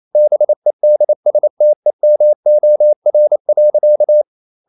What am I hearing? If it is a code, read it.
Morse code, heard as a keyed tone, about 28 wpm, 600 Hz. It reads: BEDSTEMOR.
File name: Ferie QRQ 28wpm